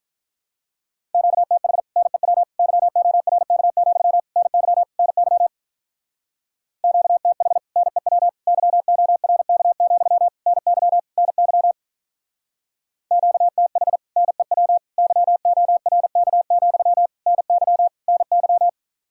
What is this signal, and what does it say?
Morse code, heard as a keyed tone, about 44 wpm, 700 Hz. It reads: QTH NEW YORK, NY NY QTH NEW YORK, NY NY QTH NEW YORK, NY NY